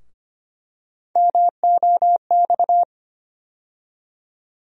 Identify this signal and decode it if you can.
Morse code, heard as a keyed tone, about 25 wpm, 700 Hz. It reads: MOX
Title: USA Call Sign Postfixes - 3 Letters 25wpm